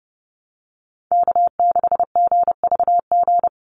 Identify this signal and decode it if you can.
Morse code, heard as a keyed tone, about 30 wpm, 700 Hz. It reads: K6GVG
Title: Call Signs Repeated 3x - 30wpm